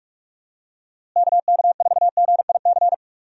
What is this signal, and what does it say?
Morse code, heard as a keyed tone, about 46 wpm, 700 Hz. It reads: KK4GIC
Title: USA Call Signs - 6 Characters - Repeated 2x - 45wpm